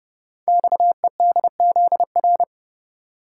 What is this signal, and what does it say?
Morse code, heard as a keyed tone, about 29 wpm, 700 Hz. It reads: XEDZR
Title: Code Groups - 5 Letters - 30wpm